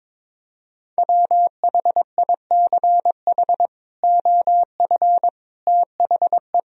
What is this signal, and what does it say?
Morse code, heard as a keyed tone, about 22 wpm, 700 Hz. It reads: WHICH OF THE
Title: Sentences from Top 1000 Words - Review Part 1 - 22wpm